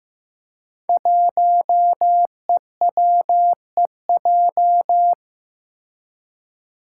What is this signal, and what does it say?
Morse code, heard as a keyed tone, about 15 wpm, 700 Hz. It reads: 1EWEJ
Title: Five Challenging Letter-Number Combinations 15wpm